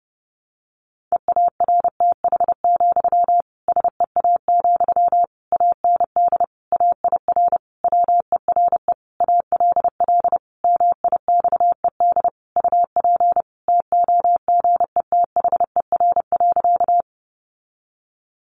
Morse code, heard as a keyed tone, about 30 wpm, 700 Hz. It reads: EARTH, SEA, AND AIR WERE ALL MIXED UP TOGETHER.
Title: Sentences from Top 1500 Words 30wpm